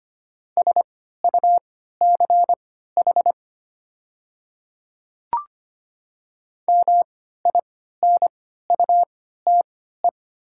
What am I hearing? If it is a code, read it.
Morse code, heard as a keyed tone, about 25 wpm, spaced as for 14 wpm, 690 Hz. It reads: SUCH MINUTE
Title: 3x Character Spacing - Top 500 Words - 25wpm